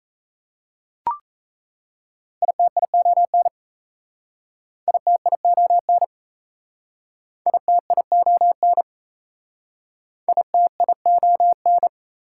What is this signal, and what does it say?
Morse code, heard as a keyed tone, about 37 wpm, 700 Hz. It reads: ITION ITION ITION ITION